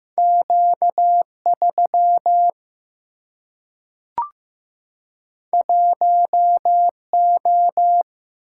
Morse code, heard as a keyed tone, about 15 wpm, 700 Hz. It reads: Q 3 1 O
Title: Single-Letter Single-Number 15wpm